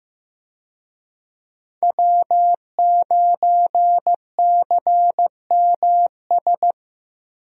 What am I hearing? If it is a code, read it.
Morse code, heard as a keyed tone, about 16 wpm, 700 Hz. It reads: W9CMS
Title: USA Call Signs - 5 Characters - 15wpm